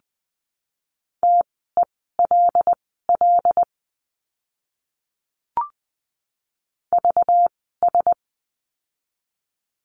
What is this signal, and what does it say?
Morse code, heard as a keyed tone, about 20 wpm, spaced as for 14 wpm, 700 Hz. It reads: TELL VS
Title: 192 Lesson 31f - Letter v - Words 20wpm